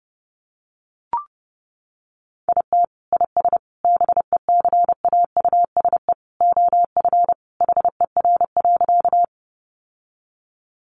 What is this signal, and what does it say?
Morse code, heard as a keyed tone, about 30 wpm, 700 Hz. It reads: IT IS BECAUSE OF HER.
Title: Sentences from Top 600 Words 30wpm